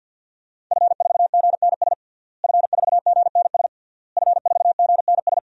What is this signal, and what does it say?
Morse code, heard as a keyed tone, about 51 wpm, 700 Hz. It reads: F4GNS F4GNS F4GNS
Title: DXCC-100 Call Signs - 5 Characters - Repeated 3x - 50wpm